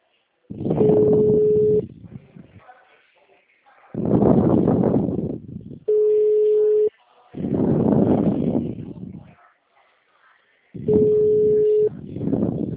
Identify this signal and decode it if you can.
Ringback tone (Europe, 425 Hz).